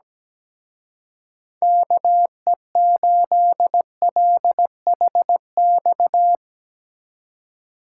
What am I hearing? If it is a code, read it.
Morse code, heard as a keyed tone, about 18 wpm, 700 Hz. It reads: KE8LHX
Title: Speed Racing - USA Call Signs 17wpm